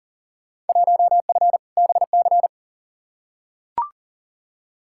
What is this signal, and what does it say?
Morse code, heard as a keyed tone, about 40 wpm, 700 Hz. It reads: 1F BC
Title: Field Day Exchanges Encore - 40wpm